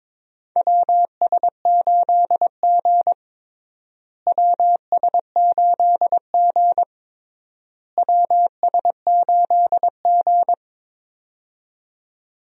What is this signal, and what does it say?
Morse code, heard as a keyed tone, about 22 wpm, 700 Hz. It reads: WS8G WS8G WS8G
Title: USA Call Signs - 4 Characters - Repeated 3x - 22wpm